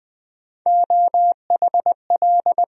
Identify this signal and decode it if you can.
Morse code, heard as a keyed tone, about 20 wpm, 700 Hz. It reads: OHL
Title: USA Call Sign Postfixes - 3 Letters 20wpm